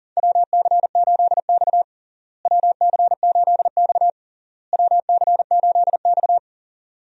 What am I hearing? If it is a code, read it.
Morse code, heard as a keyed tone, about 40 wpm, 700 Hz. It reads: WC8X WC8X WC8X
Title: USA Call Signs - 4 Characters - Repeated 3x - 40wpm